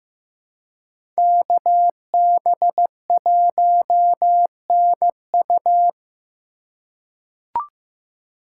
Morse code, heard as a keyed tone, about 15 wpm, 700 Hz. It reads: KB1NU
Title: USA Call Signs - 3x Character Spacing - 15wpm